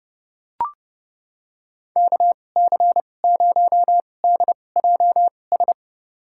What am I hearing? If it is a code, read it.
Morse code, heard as a keyed tone, about 30 wpm, spaced as for 22 wpm, 700 Hz. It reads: KC0DJS
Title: USA Call Signs - 6 Characters - 2x Character Spacing - 30wpm